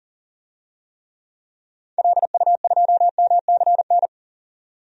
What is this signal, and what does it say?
Morse code, heard as a keyed tone, about 40 wpm, 700 Hz. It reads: LU2MCN
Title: DXCC-100 Call Signs - 6 Characters - 40wpm